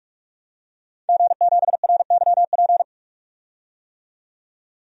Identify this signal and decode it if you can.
Morse code, heard as a keyed tone, about 42 wpm, 700 Hz. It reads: G7RYP
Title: DXCC-100 Call Signs - 5 Characters - 45wpm